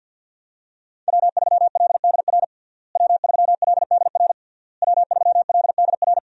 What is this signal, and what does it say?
Morse code, heard as a keyed tone, about 52 wpm, 700 Hz. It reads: W3LDR W3LDR W3LDR
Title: USA Call Signs - 5 Characters - Repeated 3x - 50wpm